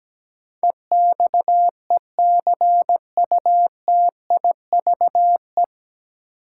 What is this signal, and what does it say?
Morse code, heard as a keyed tone, about 17 wpm, 700 Hz. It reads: EXECUTIVE
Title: Sets of 5 Words Encore - New Word Drill - 17wpm